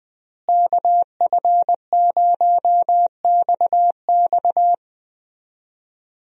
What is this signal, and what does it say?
Morse code, heard as a keyed tone, about 21 wpm, 700 Hz. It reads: KF0XX
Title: USA Call Signs - 5 Characters - Repeated 2x - 20wpm